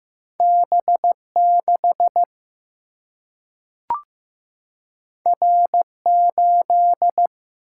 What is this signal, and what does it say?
Morse code, heard as a keyed tone, about 15 wpm, 700 Hz. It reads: B6 R8
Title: Single-Letter Single-Number 15wpm